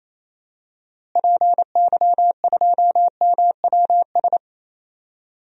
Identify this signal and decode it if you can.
Morse code, heard as a keyed tone, about 27 wpm, 700 Hz. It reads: PY2MWS
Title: DXCC-100 Call Signs - 6 Characters - Repeated 2x - 28wpm